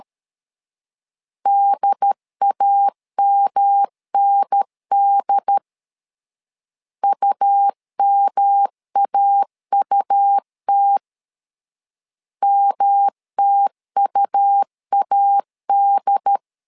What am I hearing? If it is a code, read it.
Morse code, heard as a keyed tone, about 13 wpm, 800 Hz. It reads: DAMND UMAUT MTUAD
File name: LEZIONI / 3 MWRO / 3 M 12wpm 1541